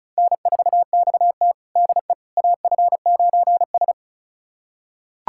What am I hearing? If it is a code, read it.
Morse code, heard as a keyed tone, about 36 wpm, 700 Hz. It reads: N4XT DE AF9S
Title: USA Callsign DE Callsign - 4 Characters - 35wpm